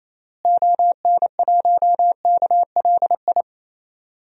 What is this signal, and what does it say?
Morse code, heard as a keyed tone, about 28 wpm, 700 Hz. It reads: ON1KLI
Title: DXCC-100 Call Signs - 6 Characters - Repeated 3x - 28wpm